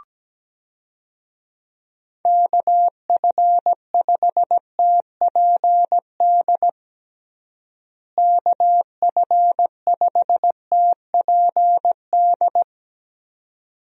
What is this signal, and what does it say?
Morse code, heard as a keyed tone, about 17 wpm, 700 Hz. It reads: KF5TPD KF5TPD
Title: USA Call Signs - 6 Characters - Repeated 2x - 17wpm